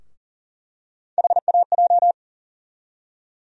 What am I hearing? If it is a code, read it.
Morse code, heard as a keyed tone, about 40 wpm, 700 Hz. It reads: HAJ
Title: USA Call Sign Postfixes - 3 Letters 40wpm